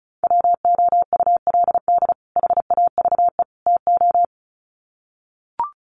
Morse code, heard as a keyed tone, about 34 wpm, 700 Hz. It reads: WOULD HAVE TO
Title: Sets of 3 Words 35wpm